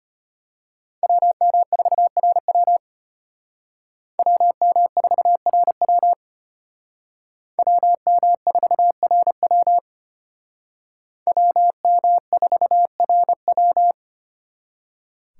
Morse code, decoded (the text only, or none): WM4RW WM4RW WM4RW WM4RW